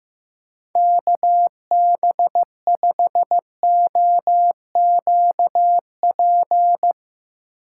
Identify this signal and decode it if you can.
Morse code, heard as a keyed tone, about 15 wpm, 700 Hz. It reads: KB5OQP
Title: USA Call Signs - 6 Characters - 8x Character Spacing - 15wpm